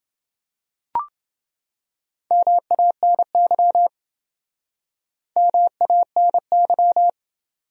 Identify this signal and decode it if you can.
Morse code, heard as a keyed tone, about 28 wpm, 700 Hz. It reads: MANY MANY